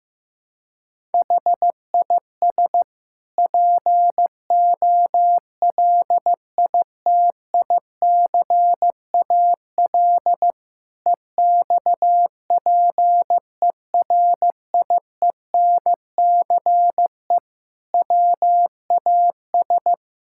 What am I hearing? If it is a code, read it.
Morse code, heard as a keyed tone, about 15 wpm, 700 Hz. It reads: HIS POLITICAL EXPERIENCE WAS
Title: Sentences from Top 1000 Words - Review Part 6 - 15wpm